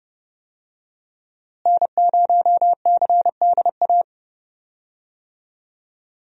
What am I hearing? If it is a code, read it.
Morse code, heard as a keyed tone, about 30 wpm, 700 Hz. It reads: N0CDA
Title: USA Call Signs - 5 Characters - Repeated 3x - 30wpm